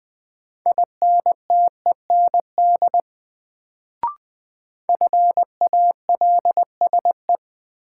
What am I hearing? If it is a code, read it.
Morse code, heard as a keyed tone, about 20 wpm, 700 Hz. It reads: INTEND FALSE
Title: Sentences from Top 1600 Words - New Word Drill - 20wpm